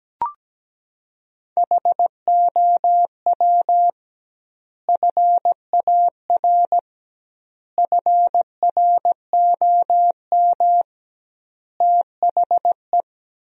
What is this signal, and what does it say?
Morse code, heard as a keyed tone, about 17 wpm, 700 Hz. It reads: HOW FAR FROM THE
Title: 2x Word-Spacing - Sentences from Top 400 Words - 17wpm